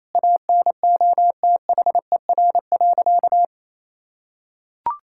Morse code, heard as a keyed tone, about 28 wpm, 700 Hz. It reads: ANOTHER.
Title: Sentences from Top 1600 Words 28wpm